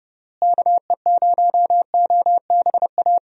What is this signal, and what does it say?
Morse code, heard as a keyed tone, about 32 wpm, 700 Hz. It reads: KE0OBA
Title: USA Call Signs - 6 Characters - Repeated 2x - 30wpm